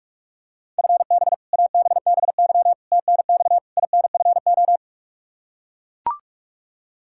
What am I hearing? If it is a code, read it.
Morse code, heard as a keyed tone, about 42 wpm, 700 Hz. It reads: FB ABBY TNX INFO
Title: Practice QSO - Exchange 3 - More Info - 45wpm